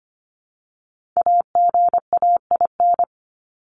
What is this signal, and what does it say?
Morse code, heard as a keyed tone, about 25 wpm, 700 Hz. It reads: AGAIN
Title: Top 200 Words 25wpm